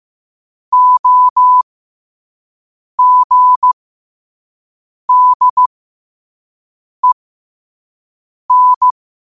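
Morse code, heard as a keyed tone, about 15 wpm, spaced as for 5 wpm, 1000 Hz. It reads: OGDEN
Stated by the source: Morse code sent at 5 wpm.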